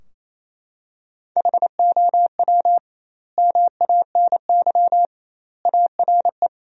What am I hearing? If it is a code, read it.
Morse code, heard as a keyed tone, about 28 wpm, 700 Hz. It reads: HOW MANY ARE
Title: 2x Word-Spacing - Sentences from Top 300 Words - 28wpm